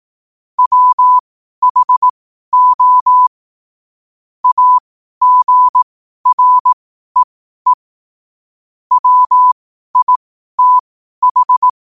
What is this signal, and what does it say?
Morse code, heard as a keyed tone, about 18 wpm, spaced as for 12 wpm, 1000 Hz. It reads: WHO AGREE WITH
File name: qotdfiles / 12wpm